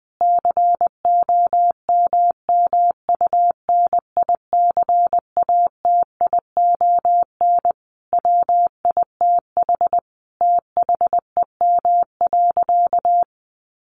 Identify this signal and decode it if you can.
Morse code, heard as a keyed tone, about 20 wpm, 700 Hz. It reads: COMMUNICATION WITH THEM.
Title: Sentences from Top 1400 Words 20wpm